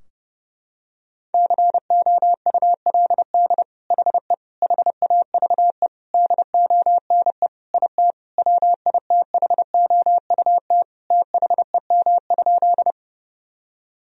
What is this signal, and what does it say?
Morse code, heard as a keyed tone, about 31 wpm, 700 Hz. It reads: COULD HE HAVE DONE IT WITHOUT THEM?
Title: Sentences from Top 1000 Words - Review Part 3 - 30wpm